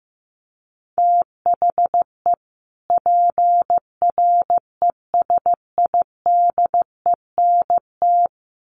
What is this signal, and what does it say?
Morse code, heard as a keyed tone, about 15 wpm, 700 Hz. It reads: THE PRESIDENT
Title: Sets of 2 Words 15wpm